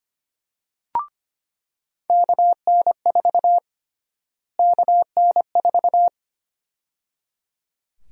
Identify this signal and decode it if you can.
Morse code, heard as a keyed tone, about 25 wpm, 700 Hz. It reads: KN4 KN4
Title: USA Call Sign Prefixes - Repeated 2x - 25wpm